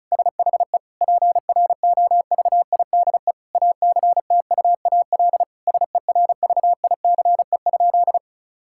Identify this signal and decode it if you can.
Morse code, heard as a keyed tone, about 36 wpm, 700 Hz. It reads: SHE PROVIDE ACTUAL SERVICE?
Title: Sentences from Top 1000 Words - Review Part 5 - 35wpm